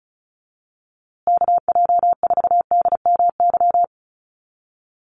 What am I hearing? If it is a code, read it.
Morse code, heard as a keyed tone, about 34 wpm, 700 Hz. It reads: KJ4DMY
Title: USA Call Signs Repeated 2x 35wpm